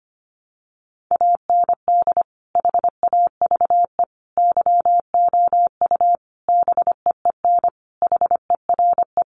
Morse code, heard as a keyed tone, about 25 wpm, 700 Hz. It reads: AND HAVE YOU BEEN HERE
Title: Sentences from Top 200 Words 25wpm